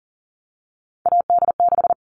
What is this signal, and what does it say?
Morse code, heard as a keyed tone, about 40 wpm, 700 Hz. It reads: AD6
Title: All Cut Numbers - 3-Digit 40wpm